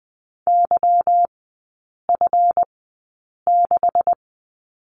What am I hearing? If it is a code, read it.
Morse code, heard as a keyed tone, about 20 wpm, spaced as for 8 wpm, 700 Hz. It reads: YF6
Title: 184 Lesson 30e - Letter b - Three Characters 20wpm